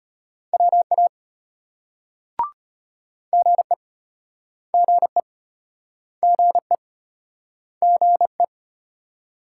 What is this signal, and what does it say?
Morse code, heard as a keyed tone, about 34 wpm, 700 Hz. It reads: WA GE GE GE GE